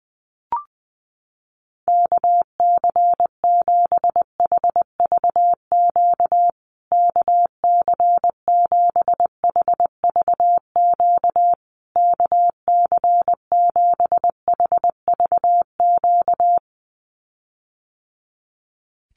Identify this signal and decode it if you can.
Morse code, heard as a keyed tone, about 20 wpm, 700 Hz. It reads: KC7HVQ KC7HVQ KC7HVQ
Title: USA Call Signs Repeated 3x 20wpm